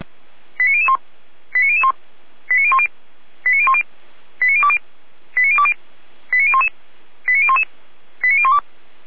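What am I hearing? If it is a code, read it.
Five-tone selective call (ZVEI), heard as quick runs of five tones, tones of about 70 ms. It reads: 89001 89001 89010 89010 89020 89020 89011 89011 89012
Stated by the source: ZVEI five-tone selective call received on an SDR.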